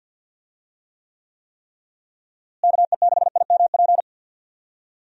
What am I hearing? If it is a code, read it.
Morse code, heard as a keyed tone, about 50 wpm, 700 Hz. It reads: KE6IMP
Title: USA Call Signs - 6 Characters - Repeated 3x - 50wpm